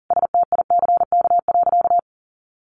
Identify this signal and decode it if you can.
Morse code, heard as a keyed tone, about 40 wpm, 700 Hz. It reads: STICK.
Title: Sentences from Top 1200 Words 40wpm